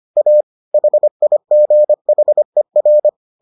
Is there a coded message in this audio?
Yes: Morse code from 0.2 to 3.1 s, heard as a keyed tone, about 25 wpm, 600 Hz. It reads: A HIGHER